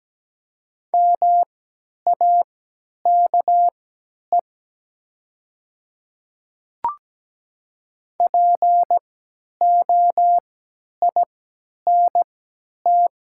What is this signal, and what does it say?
Morse code, heard as a keyed tone, about 17 wpm, spaced as for 10 wpm, 700 Hz. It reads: MAKE POINT
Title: 3x Character Spacing - Top 200 Words - 17wpm